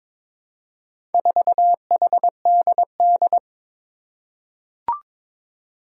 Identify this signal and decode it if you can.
Morse code, heard as a keyed tone, about 22 wpm, 700 Hz. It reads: 4HDD
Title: Four Challenging Letter-Number Combinations 22wpm